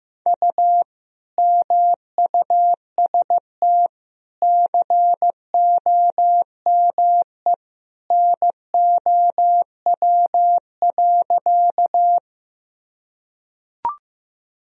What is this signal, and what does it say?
Morse code, heard as a keyed tone, about 15 wpm, 700 Hz. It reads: U MUST COME NOW.
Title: Sentences from Top 200 Words and Abbreviations 15wpm